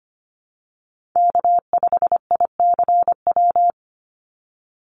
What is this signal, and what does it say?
Morse code, heard as a keyed tone, about 25 wpm, 700 Hz. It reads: K5ICW
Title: USA Call Signs 25wpm